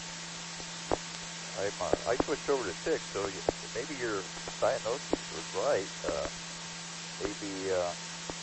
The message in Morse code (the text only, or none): none